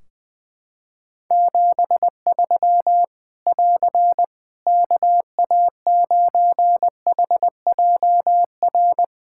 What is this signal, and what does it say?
Morse code, heard as a keyed tone, about 19 wpm, 700 Hz. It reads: 73 <AR> KA9HJR
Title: Practice QSO - Exchange 4 - Ending - 20wpm